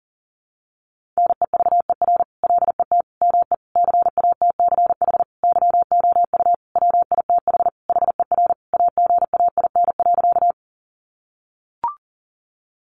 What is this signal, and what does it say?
Morse code, heard as a keyed tone, about 40 wpm, 700 Hz. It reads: NEVER LET ME CATCH YOU WITH HER AGAIN.
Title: Sentences from Top 1500 Words 40wpm